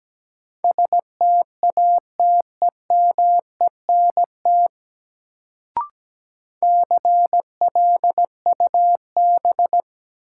Morse code, heard as a keyed tone, about 17 wpm, 700 Hz. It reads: STATEMENT CLUB
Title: Sentences from Top 1200 Words - New Word Drill Alt - 17wpm